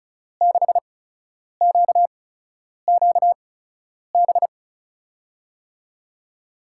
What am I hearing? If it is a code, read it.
Morse code, heard as a keyed tone, about 36 wpm, spaced as for 10 wpm, 700 Hz. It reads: 6QQB
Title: USA Call Signs - 6 Characters - 8x Character Spacing - 35wpm